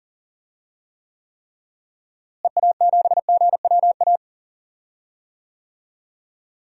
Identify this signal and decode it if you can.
Morse code, heard as a keyed tone, about 40 wpm, 700 Hz. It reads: EA7GWA
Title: DXCC-100 Call Signs - 6 Characters - Repeated 3x - 40wpm